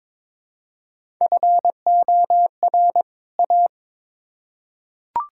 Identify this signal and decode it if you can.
Morse code, heard as a keyed tone, about 22 wpm, 700 Hz. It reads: FOR A
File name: Build A Phrase - 2-Word Phrases - 22wpm